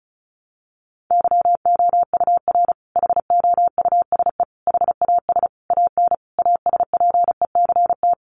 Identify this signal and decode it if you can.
Morse code, heard as a keyed tone, about 34 wpm, 700 Hz. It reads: YOUR HOUSE HAS AN ASPECT
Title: Sentences from Top 1300 Words 35wpm